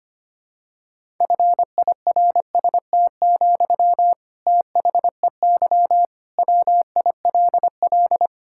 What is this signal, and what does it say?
Morse code, heard as a keyed tone, about 25 wpm, 700 Hz. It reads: FIRST, THEY WILL
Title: Sentences from Top 1000 Words - Review Part 6 - 25wpm